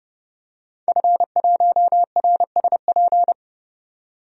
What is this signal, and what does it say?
Morse code, heard as a keyed tone, about 29 wpm, 700 Hz. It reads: F1RSP
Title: DXCC-100 Call Signs - 5 Characters - 30wpm